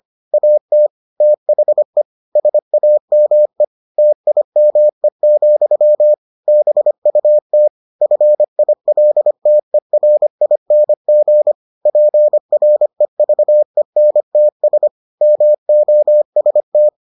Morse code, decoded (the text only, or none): AT THE SAME TIME, BUT FILTERING PREVENTS MOST